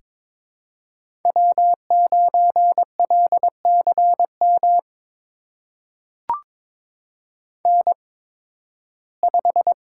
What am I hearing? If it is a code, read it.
Morse code, heard as a keyed tone, about 22 wpm, spaced as for 14 wpm, 700 Hz. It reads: W9LCM N5
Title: USA Call Signs - 5 Characters - 8x Character Spacing - 22wpm